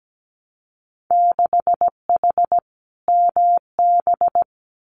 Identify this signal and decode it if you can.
Morse code, heard as a keyed tone, about 17 wpm, 700 Hz. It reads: 6H MB
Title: Winter Field Day Exchanges Encore 17wpm